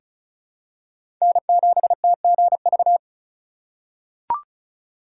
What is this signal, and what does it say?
Morse code, heard as a keyed tone, about 36 wpm, 700 Hz. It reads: N7TGV
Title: USA Call Signs - 5 Characters - 3x Character Spacing - 35wpm